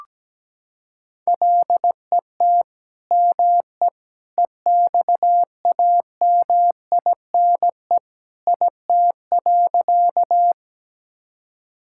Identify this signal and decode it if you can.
Morse code, heard as a keyed tone, about 17 wpm, 700 Hz. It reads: LET ME EXAMINE IT.
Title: Sentences from Top 1600 Words 17wpm